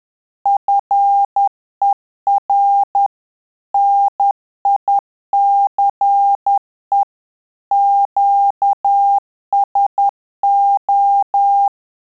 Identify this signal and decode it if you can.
Morse code, heard as a keyed tone, about 10 wpm, 800 Hz. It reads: FER NICE QSO
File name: Telegraphy-group / QSO CW 50 CPM